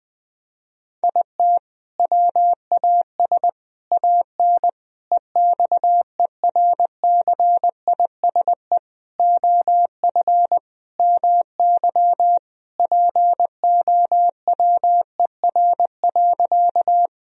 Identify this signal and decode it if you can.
Morse code, heard as a keyed tone, about 21 wpm, 700 Hz. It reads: IT WAS AN EXERCISE OF MY POWER.
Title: Sentences from Top 1000 Words - Review Part 4 - 20wpm